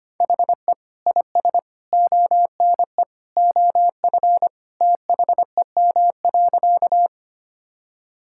Morse code, heard as a keyed tone, about 25 wpm, 700 Hz. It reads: HE IS ONE OF THEM.
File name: practice / 25wpm / Sentences with 3-Word Phrases - 25wpm